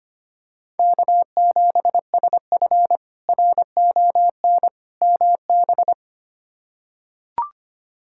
Morse code, heard as a keyed tone, about 25 wpm, 700 Hz. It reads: K7SF RON MB
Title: K1USN SST QSO Practice - 25wpm